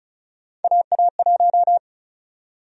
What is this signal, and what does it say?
Morse code, heard as a keyed tone, about 34 wpm, 700 Hz. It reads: AA1
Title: USA Call Sign Prefixes - 35wpm